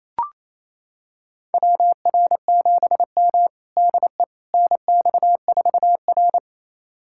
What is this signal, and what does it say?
Morse code, heard as a keyed tone, about 27 wpm, 700 Hz. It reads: WR7M DE NX4R
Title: USA Callsign DE Callsign - 4 Characters - 28wpm